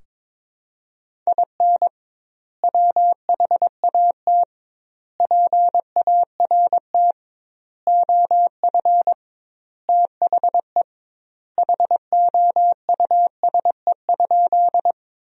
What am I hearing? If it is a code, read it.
Morse code, heard as a keyed tone, about 21 wpm, 700 Hz. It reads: IN WHAT PART OF THE HOUSE?
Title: 2x Word-Spacing - Sentences from Top 200 Words - 22wpm